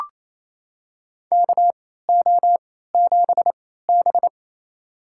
Morse code, heard as a keyed tone, about 28 wpm, spaced as for 16 wpm, 690 Hz. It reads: KO7B
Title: USA Call Signs - 3x Character Spacing - 28wpm